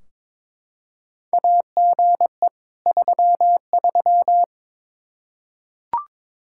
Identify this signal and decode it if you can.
Morse code, heard as a keyed tone, about 21 wpm, 700 Hz. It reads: AGE 33
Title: Practice QSO - Exchange 3 - More Info - 22wpm